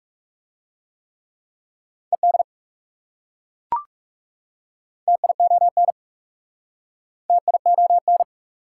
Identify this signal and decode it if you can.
Morse code, heard as a keyed tone, about 39 wpm, 700 Hz. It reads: ED TION TION